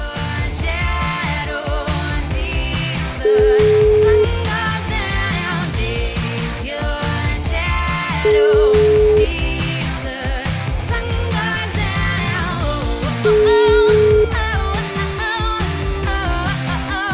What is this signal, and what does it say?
Ringback tone (Europe, 425 Hz), with noise.